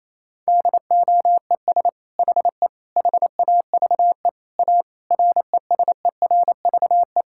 Morse code, heard as a keyed tone, about 28 wpm, 700 Hz. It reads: DOES HE HAVE A RESERVE
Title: Sentences from Top 1000 Words - Review Part 6 - 28wpm